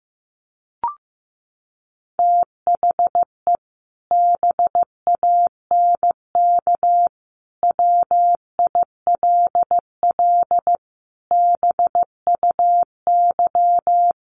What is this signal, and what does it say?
Morse code, heard as a keyed tone, about 15 wpm, 700 Hz. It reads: THE BANK WILL BUY
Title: Sentences from Top 1500 Words 15wpm